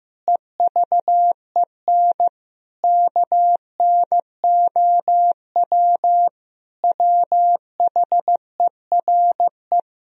Morse code, heard as a keyed tone, about 15 wpm, 700 Hz. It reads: EVEN KNOW WHERE
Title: Sentences from Top 1000 Words - Review Part 1 - 15wpm